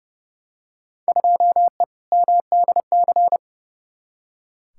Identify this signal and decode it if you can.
Morse code, heard as a keyed tone, about 30 wpm, 700 Hz. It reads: 2E MDC
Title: Field Day Exchanges - 30wpm